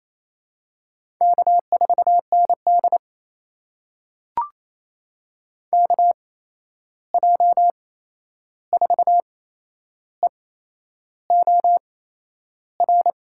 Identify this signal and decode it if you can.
Morse code, heard as a keyed tone, about 28 wpm, spaced as for 11 wpm, 700 Hz. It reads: K4ND KJ4EOR
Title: USA Call Signs - 8x Character Spacing - 28wpm